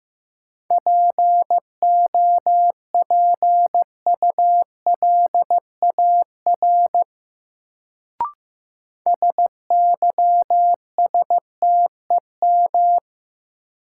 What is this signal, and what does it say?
Morse code, heard as a keyed tone, about 15 wpm, 700 Hz. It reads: POPULAR SYSTEM
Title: Sentences from Top 1000 Words - Review Part 3 - New Word Drill - 15wpm